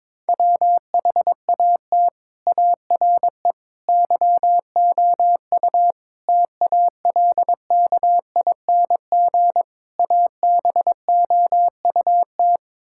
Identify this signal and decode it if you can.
Morse code, heard as a keyed tone, about 22 wpm, 700 Hz. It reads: WHAT ARE YOU TALKING ABOUT
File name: practice / 22wpm / Build A Phrase - 5-Word Phrases - 22wpm